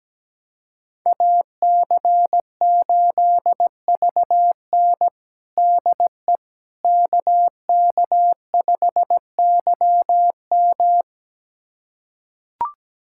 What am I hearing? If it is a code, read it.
Morse code, heard as a keyed tone, about 17 wpm, 700 Hz. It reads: AC8VN DE KK5YM
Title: USA Callsign DE Callsign - 5 Characters - 17wpm